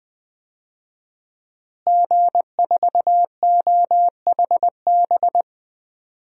Morse code, heard as a keyed tone, about 20 wpm, 700 Hz. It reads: G4OHB
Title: DXCC-100 Call Signs - 5 Characters - Repeated 2x - 20wpm